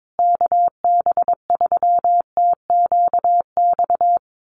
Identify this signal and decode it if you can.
Morse code, heard as a keyed tone, about 22 wpm, 700 Hz. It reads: KB3TQX
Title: USA Call Signs Repeated 2x 22wpm